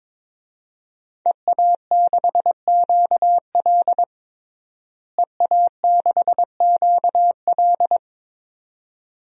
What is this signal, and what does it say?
Morse code, heard as a keyed tone, about 22 wpm, 700 Hz. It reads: EA6QL EA6QL
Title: DXCC-100 Call Signs - 5 Characters - Repeated 2x - 22wpm